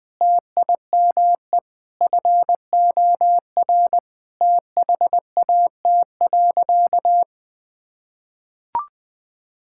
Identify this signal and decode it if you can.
Morse code, heard as a keyed tone, about 21 wpm, 700 Hz. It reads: TIME FOR THAT.
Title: Sentences with 5-Word Phrases - 20wpm